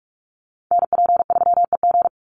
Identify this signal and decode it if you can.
Morse code, heard as a keyed tone, about 44 wpm, 700 Hz. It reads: NP3EG
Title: USA Call Signs 45wpm